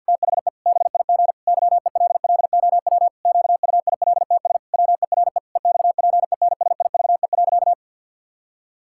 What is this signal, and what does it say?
Morse code, heard as a keyed tone, about 51 wpm, 700 Hz. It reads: THE BIG YELLOW QUILTS WERE EXPENSIVE.
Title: Pangram Challenge v2 - 50wpm